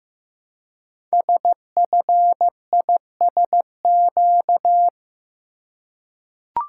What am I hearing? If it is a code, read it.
Morse code, heard as a keyed tone, about 15 wpm, 700 Hz. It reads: SFISQ
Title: Code Groups - 5 Letter:Numbers - 15wpm